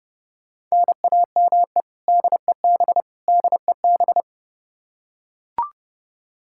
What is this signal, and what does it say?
Morse code, heard as a keyed tone, about 32 wpm, 700 Hz. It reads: NAME DEB DEB
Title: Practice QSO - Exchange 1 - Beginning - 30wpm